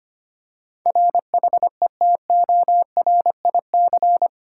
Morse code, heard as a keyed tone, about 25 wpm, 700 Hz. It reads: RHETORIC
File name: Eight Letter Words - 25wpm